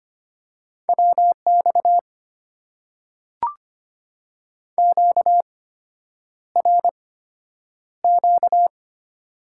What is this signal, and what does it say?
Morse code, heard as a keyed tone, about 25 wpm, spaced as for 9 wpm, 700 Hz. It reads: WX QRQ
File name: Top 90 QSO Elements - 8x Character Spacing - 25wpm